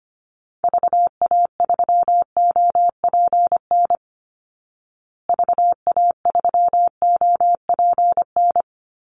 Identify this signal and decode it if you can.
Morse code, heard as a keyed tone, about 25 wpm, 700 Hz. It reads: VA3OPN VA3OPN
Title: Call Signs Repeated 3x - 25wpm